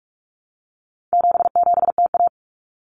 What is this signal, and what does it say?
Morse code, heard as a keyed tone, about 44 wpm, 700 Hz. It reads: 77TA
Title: Common Cut Numbers - 4-Digit 45wpm